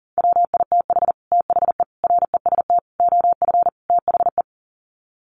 Morse code, heard as a keyed tone, about 40 wpm, 700 Hz. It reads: WITH THE REST OF THE
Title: Sets of 5 Words 40wpm